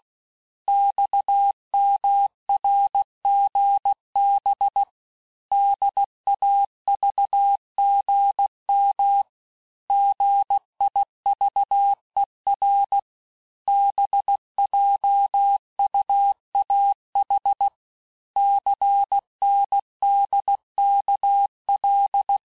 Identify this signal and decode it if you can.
Morse code, heard as a keyed tone, about 16 wpm, 800 Hz. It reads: XMRGB DAVGM GIVER BJUAH CNDKL
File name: cours-du-lundi files / Cours-1586-texte-chiffre-vitesse-15mpm